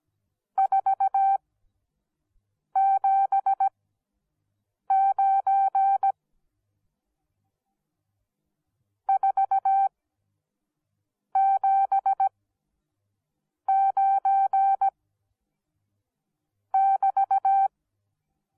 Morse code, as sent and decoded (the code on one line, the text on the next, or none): ....- --... ----. / ....- --... ----. / -...-
479 479 <BT>